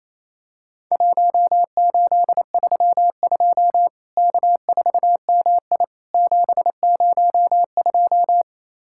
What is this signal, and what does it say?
Morse code, heard as a keyed tone, about 28 wpm, 700 Hz. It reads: 1832 K4MI 702
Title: Worked All Europe DX Contest - QTC Exchange - 28wpm